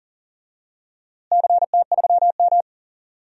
Morse code, heard as a keyed tone, about 40 wpm, 700 Hz. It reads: CT3M
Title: DXCC-100 Call Signs - 4 Characters - Repeated 2x - 40wpm